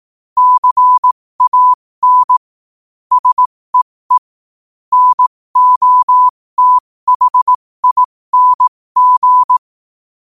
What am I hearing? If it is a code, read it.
Morse code, heard as a keyed tone, about 18 wpm, spaced as for 15 wpm, 1000 Hz. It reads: CAN SEE NOTHING
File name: qotdfiles / 15wpm